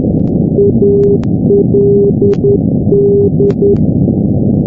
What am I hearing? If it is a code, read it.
Morse code, heard as a keyed tone, about 10 wpm, 400 Hz, with noise. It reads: ALD
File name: europe / g-910-ald-383-sdr-hu-g-south west uk- -1245z